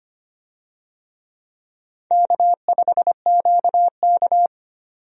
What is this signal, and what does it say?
Morse code, heard as a keyed tone, about 25 wpm, 700 Hz. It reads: K5QK
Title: USA Call Signs - 4 Characters - 25wpm